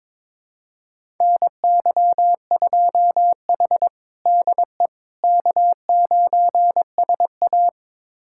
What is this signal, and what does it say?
Morse code, heard as a keyed tone, about 22 wpm, 700 Hz. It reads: NY2H DE K9SA
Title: USA Callsign DE Callsign - 4 Characters - 22wpm